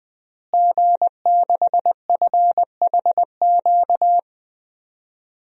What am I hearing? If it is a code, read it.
Morse code, heard as a keyed tone, about 20 wpm, 700 Hz. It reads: G6FHQ
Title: DXCC-100 Call Signs - 5 Characters - Repeated 3x - 20wpm